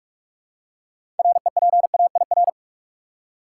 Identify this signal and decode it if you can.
Morse code, heard as a keyed tone, about 46 wpm, 700 Hz. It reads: REPAIR
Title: Six Letter Words - 45wpm